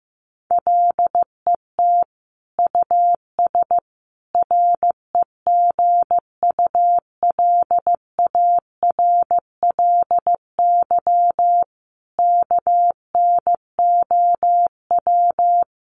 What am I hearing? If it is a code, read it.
Morse code, heard as a keyed tone, about 15 wpm, 700 Hz. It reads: LET US REGULARLY KNOW
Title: Sentences from Top 1400 Words 15wpm